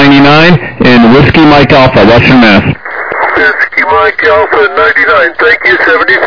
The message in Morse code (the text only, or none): none